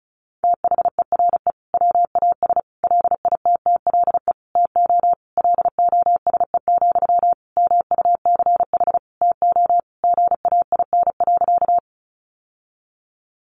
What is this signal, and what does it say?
Morse code, heard as a keyed tone, about 34 wpm, 700 Hz. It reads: THERE WAS LITTLE TO LOSE, MUCH TO GAIN.
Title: Sentences from Top 1100 Words 35wpm